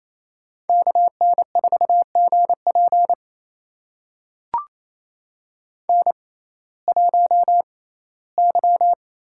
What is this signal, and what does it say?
Morse code, heard as a keyed tone, about 28 wpm, spaced as for 17 wpm, 700 Hz. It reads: KN4GP N1Y
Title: USA Call Signs - 5 Characters - 6x Character Spacing - 28wpm